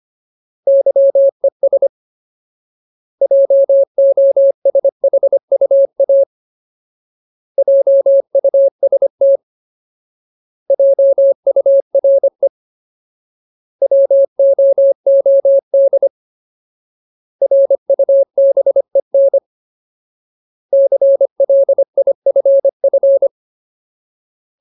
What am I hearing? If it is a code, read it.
Morse code, heard as a keyed tone, about 25 wpm, 550 Hz. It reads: YES JOSHUA JUST JURE WOOD RUBEN CLIFF
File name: Ultimate Word List / 25WPM Ultimate